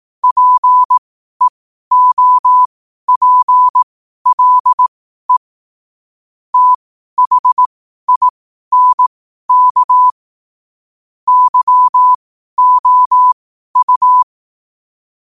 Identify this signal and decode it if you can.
Morse code, heard as a keyed tone, about 18 wpm, spaced as for 12 wpm, 1000 Hz. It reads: PEOPLE THINK YOU
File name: qotdfiles / 12wpm